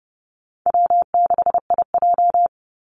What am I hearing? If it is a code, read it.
Morse code, heard as a keyed tone, about 30 wpm, 700 Hz. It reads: W6IJ
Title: Call Signs - 30wpm